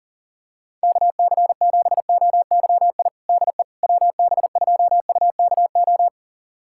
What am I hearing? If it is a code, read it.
Morse code, heard as a keyed tone, about 40 wpm, 700 Hz. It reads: KC7OYI DE WB2UKO
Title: USA Callsign DE Callsign - 6 Characters - 40wpm